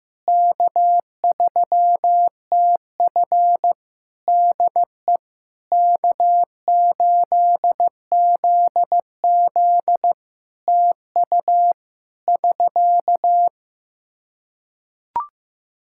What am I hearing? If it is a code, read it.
Morse code, heard as a keyed tone, about 15 wpm, 700 Hz. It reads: K3TF DE K8ZZ TU <SK>
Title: K1USN SST QSO Practice 15wpm